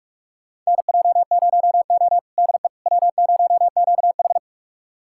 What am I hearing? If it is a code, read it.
Morse code, heard as a keyed tone, about 46 wpm, 700 Hz. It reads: NJ0O DE W0QH
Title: USA Callsign DE Callsign - 4 Characters - 45wpm